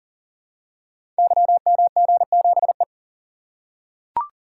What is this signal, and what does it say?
Morse code, heard as a keyed tone, about 42 wpm, 700 Hz. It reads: YMG7E
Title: Code Groups - 5 Letter:Numbers - 40wpm